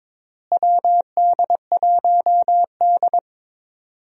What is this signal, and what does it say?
Morse code, heard as a keyed tone, about 22 wpm, 700 Hz. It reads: WD1D
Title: USA Call Signs - 4 Characters - Repeated 3x - 22wpm